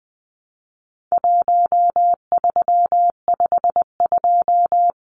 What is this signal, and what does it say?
Morse code, heard as a keyed tone, about 20 wpm, 700 Hz. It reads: 1352
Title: Numbers - Contest Style - 4-Digit 20wpm